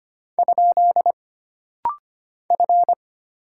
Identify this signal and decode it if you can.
Morse code, heard as a keyed tone, about 25 wpm, 700 Hz. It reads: ? F
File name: Single Character (Letter:Number:Punctuation:Prosign) - Rapid-Fire - 25wpm